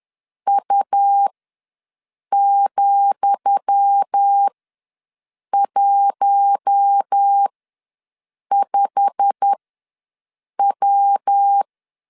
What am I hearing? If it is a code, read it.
Morse code, heard as a keyed tone, about 10 wpm, spaced as for 6 wpm, 800 Hz. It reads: U,15W
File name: LEZIONI / 10 SIGNS / 10D COMA ws 10wpm 4314